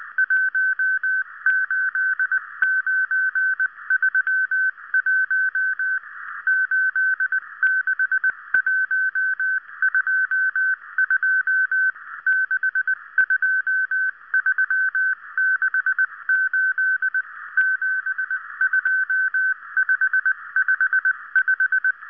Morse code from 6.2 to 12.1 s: 86122